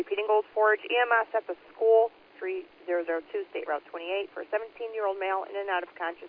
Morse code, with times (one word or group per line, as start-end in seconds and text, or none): none